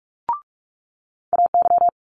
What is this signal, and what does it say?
Morse code, heard as a keyed tone, about 46 wpm, 700 Hz. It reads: AY
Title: Multi-letter Phonemes Quick Fire 45wpm